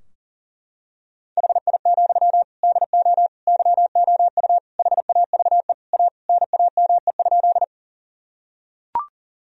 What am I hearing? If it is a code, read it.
Morse code, heard as a keyed tone, about 40 wpm, 700 Hz. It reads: HI, DO YOU HAVE A NAME?
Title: Sentences from Top 1000 Words - Review Part 3 - 40wpm